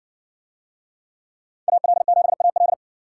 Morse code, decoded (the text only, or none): AL7AL